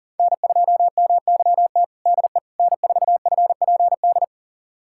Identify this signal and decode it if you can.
Morse code, heard as a keyed tone, about 40 wpm, 700 Hz. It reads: N2MYT DE N4FPD
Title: USA Callsign DE Callsign - 5 Characters - 40wpm